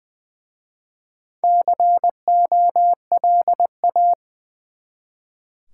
Morse code, heard as a keyed tone, about 20 wpm, 700 Hz. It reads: COLA
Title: Easy Words with Hints - 20wpm